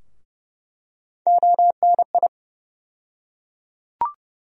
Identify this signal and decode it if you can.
Morse code, heard as a keyed tone, about 30 wpm, 700 Hz. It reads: ONI
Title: USA Call Sign Postfixes - 3 Letters 30wpm